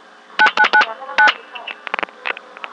DTMF, heard as telephone touch tones, tones of about 90 ms. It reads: 9999